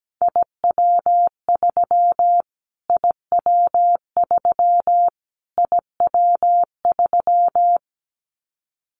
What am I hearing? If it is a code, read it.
Morse code, heard as a keyed tone, about 17 wpm, 700 Hz. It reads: IW3 IW3 IW3
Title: Callsign Prefixes for Top 25 DXCC - Repeated 3x - 17wpm